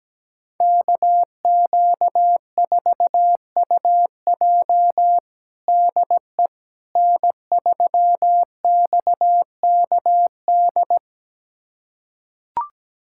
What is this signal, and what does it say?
Morse code, heard as a keyed tone, about 17 wpm, 700 Hz. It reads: KQ4UJ DE N3XKD
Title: USA Callsign DE Callsign - 5 Characters - 17wpm